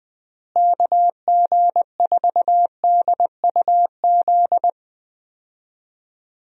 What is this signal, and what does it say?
Morse code, heard as a keyed tone, about 21 wpm, 700 Hz. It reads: KG4DUZ
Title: USA Call Signs - 6 Characters - Repeated 3x - 20wpm